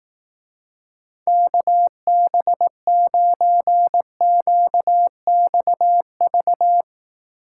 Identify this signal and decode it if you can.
Morse code, heard as a keyed tone, about 18 wpm, 700 Hz. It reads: KB9QXV
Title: Speed Racing - USA 2x3 Call Signs 15wpm